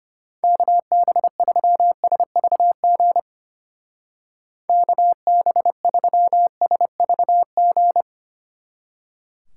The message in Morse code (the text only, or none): KB3SVG KB3SVG